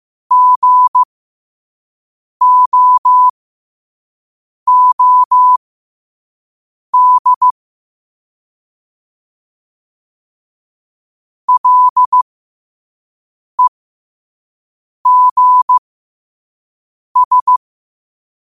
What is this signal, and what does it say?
Morse code, heard as a keyed tone, about 15 wpm, spaced as for 5 wpm, 1000 Hz. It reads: GOOD LEGS
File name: qotdfiles / 5wpm